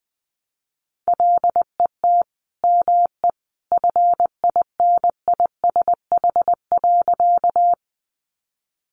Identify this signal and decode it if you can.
Morse code, heard as a keyed tone, about 20 wpm, 700 Hz. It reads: LET ME FINISH.
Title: Sentences from Top 1200 Words 20wpm